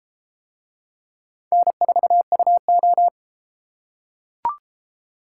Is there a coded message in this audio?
Yes: Morse code from 1.5 to 3.1 s, heard as a keyed tone, about 31 wpm, 700 Hz. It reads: N4UO